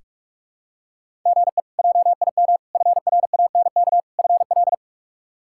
Morse code, heard as a keyed tone, about 46 wpm, 700 Hz. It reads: GE JIM FRANK FL
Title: K1USN SST QSO Practice - 45wpm